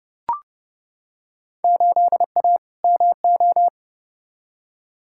Morse code, heard as a keyed tone, about 30 wpm, 700 Hz. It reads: 8A MO
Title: Field Day Exchanges Encore - 30wpm